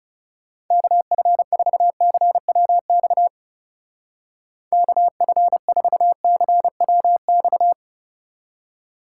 Morse code, heard as a keyed tone, about 38 wpm, 700 Hz. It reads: KF4CWX KF4CWX